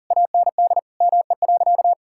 Morse code, heard as a keyed tone, about 40 wpm, 700 Hz. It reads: AND ME.
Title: Sentences from Top 1000 Words - Review Part 6 - 40wpm